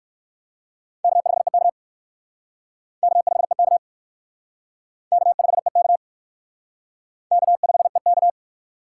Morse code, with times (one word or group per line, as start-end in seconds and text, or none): 1.0-1.7 K5EK
3.0-3.8 K5EK
5.1-6.0 K5EK
7.3-8.3 K5EK